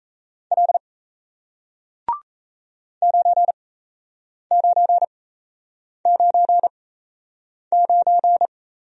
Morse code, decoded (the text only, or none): L 9999